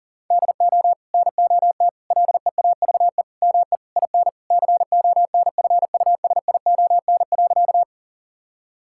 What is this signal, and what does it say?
Morse code, heard as a keyed tone, about 40 wpm, 700 Hz. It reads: DO NOT LEAVE ME IN CONFUSION.
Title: Sentences from Top 1600 Words 40wpm